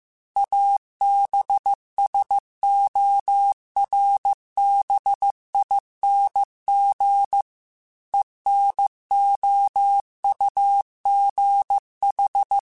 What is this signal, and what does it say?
Morse code, heard as a keyed tone, about 15 wpm, 800 Hz. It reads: ABSORBING ENOUGH